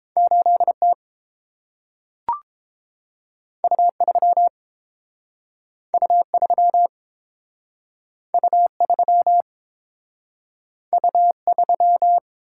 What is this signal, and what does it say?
Morse code, heard as a keyed tone, about 30 wpm, 700 Hz. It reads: 8T U3 U3 U3 U3